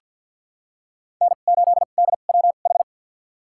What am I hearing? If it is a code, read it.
Morse code, heard as a keyed tone, about 50 wpm, spaced as for 36 wpm, 700 Hz. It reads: N8DWH
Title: USA Call Signs - 5 Characters - 2x Character Spacing - 50wpm